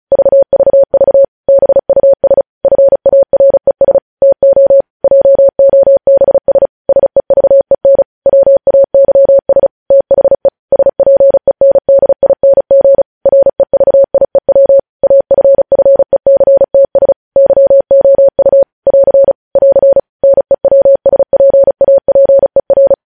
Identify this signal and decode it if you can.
Morse code, heard as a keyed tone, about 36 wpm, 550 Hz. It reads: VVV BUS FARES TO JOBS SEVEN WAYS THE SPENDING REVIEW AFFECTS YOU <AR> <AR> NEWSPAPER
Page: news headlines in morse code at 35 wpm